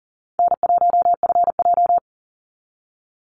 Morse code, heard as a keyed tone, about 40 wpm, 700 Hz. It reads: N1FJ
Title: NA SOTA Chaser Callsigns 40wpm